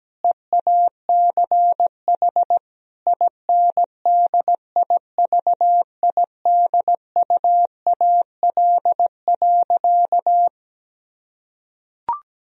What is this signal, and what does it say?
Morse code, heard as a keyed tone, about 17 wpm, 700 Hz. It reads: EACH INDIVIDUAL.
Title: Sentences from Top 1000 Words - Review Part 4 - 17wpm